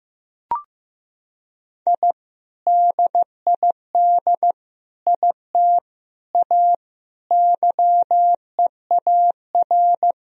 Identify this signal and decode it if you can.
Morse code, heard as a keyed tone, about 15 wpm, 700 Hz. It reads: I DID IT A YEAR
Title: Sentences from Top 1000 Words - Review Part 3 - 15wpm